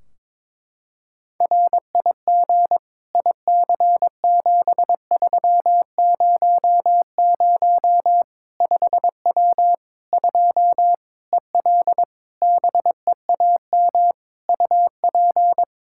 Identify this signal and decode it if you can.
Morse code, heard as a keyed tone, about 22 wpm, 700 Hz. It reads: RIG IC7300 5W 2 EL BEAM UP
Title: K1USN SST QSO Practice 22wpm